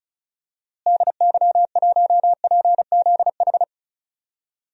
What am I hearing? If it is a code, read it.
Morse code, heard as a keyed tone, about 36 wpm, 700 Hz. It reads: DY1PZH
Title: DXCC-100 Call Signs - 6 Characters - Repeated 3x - 35wpm